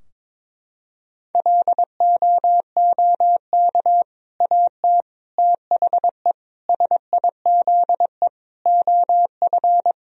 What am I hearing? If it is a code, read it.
Morse code, heard as a keyed tone, about 22 wpm, 700 Hz. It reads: LOOK AT THE SIZE OF
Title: Sentences from Top 1000 Words - Review Part 3 - 22wpm